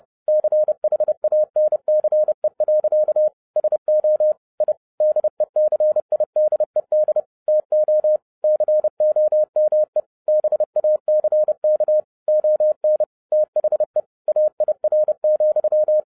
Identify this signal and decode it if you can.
Morse code, heard as a keyed tone, about 30 wpm, 610 Hz. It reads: CHANCE. SO I DECIDED TO COME BACK ON THE AIR,
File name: 2012-01-mp3 / n11-I DIDNT MISS THE SOLAR CYCLE 24-30wpm